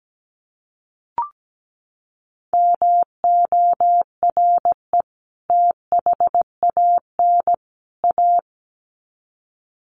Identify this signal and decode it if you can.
Morse code, heard as a keyed tone, about 17 wpm, 700 Hz. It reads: MORE THAN A
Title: Sets of 3 Words 17wpm